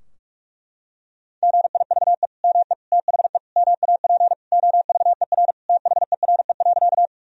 Morse code, heard as a keyed tone, about 42 wpm, 700 Hz. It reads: GIVE ME THE MAP OVER THERE.
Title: Sentences from Top 1000 Words - Review Part 2 - 45wpm